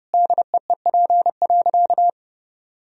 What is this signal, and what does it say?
Morse code, heard as a keyed tone, about 30 wpm, 700 Hz. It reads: DEEP.
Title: Sentences from Top 1000 Words - Review Part 3 - 30wpm